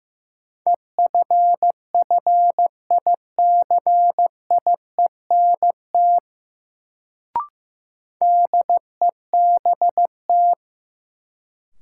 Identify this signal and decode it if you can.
Morse code, heard as a keyed tone, about 15 wpm, 700 Hz. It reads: EFFICIENT DEBT
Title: Sentences from Top 1300 Words - New Word Drill - 15wpm